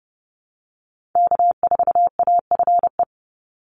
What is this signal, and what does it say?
Morse code, heard as a keyed tone, about 30 wpm, 700 Hz. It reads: K4AFE
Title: Call Signs Repeated 2x - 30wpm